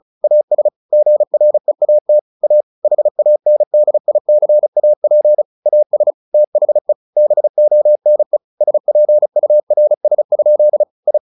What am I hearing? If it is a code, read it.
Morse code, heard as a keyed tone, about 34 wpm, 600 Hz. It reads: AS GREAT A HANDICAP AS THE BONE SPURS? I
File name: the old man and the sea 2 35wpm